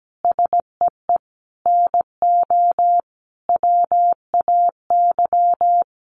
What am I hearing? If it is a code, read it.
Morse code, heard as a keyed tone, about 17 wpm, 700 Hz. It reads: SEE NO WAY
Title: Sentences from Top 1200 Words 17wpm